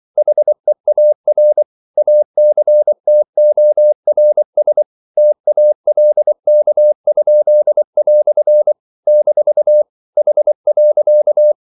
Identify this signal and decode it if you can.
Morse code, heard as a keyed tone, about 24 wpm, 600 Hz. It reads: HEAR ACTORS TALK?" - H.
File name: CW / Practice / EnglishQuotations0424